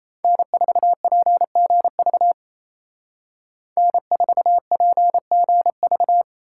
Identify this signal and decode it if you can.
Morse code, heard as a keyed tone, about 36 wpm, 700 Hz. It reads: N4PGV N4PGV